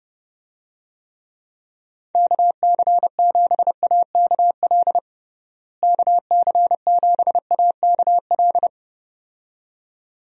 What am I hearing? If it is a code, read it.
Morse code, heard as a keyed tone, about 32 wpm, 700 Hz. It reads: KC7AKL KC7AKL
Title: USA Call Signs - 6 Characters - Repeated 2x - 30wpm